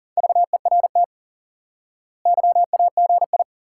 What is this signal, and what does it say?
Morse code, heard as a keyed tone, about 40 wpm, 700 Hz. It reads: VERT YAGI
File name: IQE Course 3 - Lesson 89 - YAGI - Two Elements - 40wpm